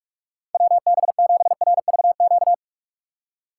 Morse code, heard as a keyed tone, about 46 wpm, 700 Hz. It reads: WB7RVQ
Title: USA Call Signs - 6 Characters - 2x Character Spacing - 45wpm